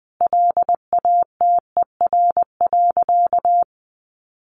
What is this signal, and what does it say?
Morse code, heard as a keyed tone, about 20 wpm, 700 Hz. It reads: LATER.
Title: Sentences from Top 700 Words 20wpm